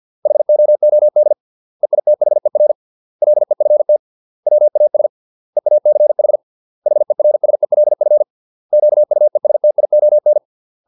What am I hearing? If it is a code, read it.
Morse code, heard as a keyed tone, about 50 wpm, 600 Hz. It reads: HOOD EITHER LEFT WAS EACH HERSELF QUESTION